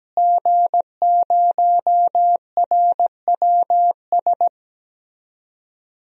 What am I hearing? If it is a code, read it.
Morse code, heard as a keyed tone, about 17 wpm, 700 Hz. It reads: G0RWS
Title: DXCC-100 Call Signs - 5 Characters - Repeated 2x - 17wpm